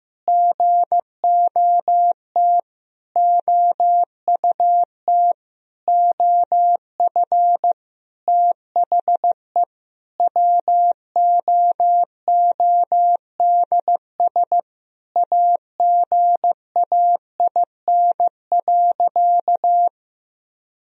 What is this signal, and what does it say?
Morse code, heard as a keyed tone, about 15 wpm, 700 Hz. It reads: GOT OUT OF THE WOODS AGAIN.
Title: Sentences from Top 1000 Words - Review Part 2 - 15wpm